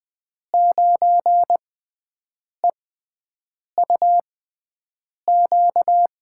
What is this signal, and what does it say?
Morse code, heard as a keyed tone, about 20 wpm, spaced as for 7 wpm, 700 Hz. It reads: 9EUQ
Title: USA Call Signs - 6 Characters - 6x Character Spacing - 20wpm